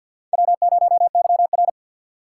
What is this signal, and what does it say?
Morse code, heard as a keyed tone, about 49 wpm, 700 Hz. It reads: W0YR
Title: K1USN SST QSO Practice - 50wpm